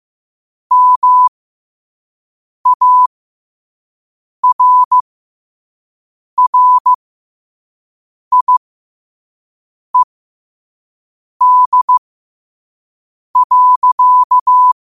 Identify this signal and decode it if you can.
Morse code, heard as a keyed tone, about 15 wpm, spaced as for 5 wpm, 1000 Hz. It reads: MARRIED.
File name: qotdfiles / 5wpm